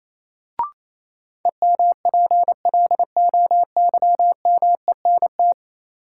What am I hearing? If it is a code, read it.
Morse code, heard as a keyed tone, about 28 wpm, 700 Hz. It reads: EMPLOYMENT
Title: Sentences from Top 1500 Words - New Word Drill Alt - 28wpm